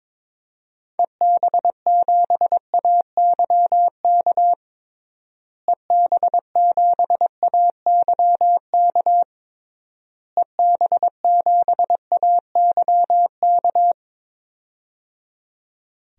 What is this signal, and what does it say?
Morse code, heard as a keyed tone, about 22 wpm, 700 Hz. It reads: EB7AYK EB7AYK EB7AYK
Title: DXCC-100 Call Signs - 6 Characters - Repeated 3x - 22wpm